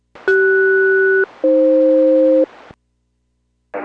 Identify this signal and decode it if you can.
Aircraft SELCAL, heard as two pairs of tones. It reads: CS-AF